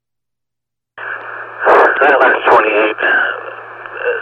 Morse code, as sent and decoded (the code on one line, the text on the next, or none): none